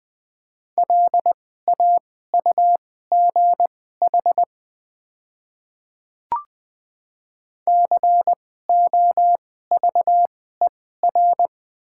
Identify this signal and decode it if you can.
Morse code, heard as a keyed tone, about 20 wpm, spaced as for 14 wpm, 700 Hz. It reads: LAUGH COVER
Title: 2x Character Spacing - Top 500 Words - 20wpm